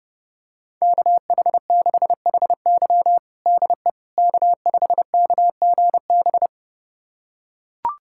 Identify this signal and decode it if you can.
Morse code, heard as a keyed tone, about 32 wpm, 700 Hz. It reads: KH6HY DE K5KGB
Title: USA Callsign DE Callsign - 5 Characters - 30wpm